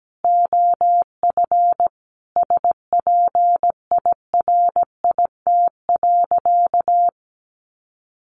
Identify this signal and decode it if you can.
Morse code, heard as a keyed tone, about 17 wpm, 700 Hz. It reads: OF SPIRIT.
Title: Sentences from Top 1400 Words 17wpm